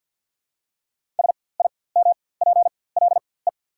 Morse code, heard as a keyed tone, about 48 wpm, spaced as for 23 wpm, 700 Hz. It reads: SIMPLE
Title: 4x Character Spacing - Top 500 Words - 50wpm